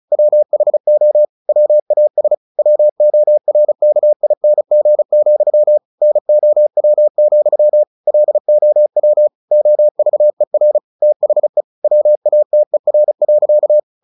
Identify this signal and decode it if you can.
Morse code, heard as a keyed tone, about 34 wpm, 600 Hz. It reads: WHO WAS WORKING, NOW, LOW OVER THE WATER.
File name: the old man and the sea / the old man and the sea 1 35wpm